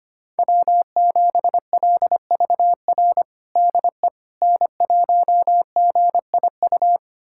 Morse code, heard as a keyed tone, about 25 wpm, 700 Hz. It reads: W7LVR DE N1GIU
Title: USA Callsign DE Callsign - 5 Characters - 25wpm